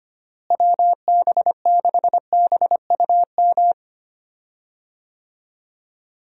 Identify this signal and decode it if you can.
Morse code, heard as a keyed tone, about 25 wpm, 700 Hz. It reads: WB6BUM
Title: USA Call Signs - 6 Characters - Repeated 2x - 25wpm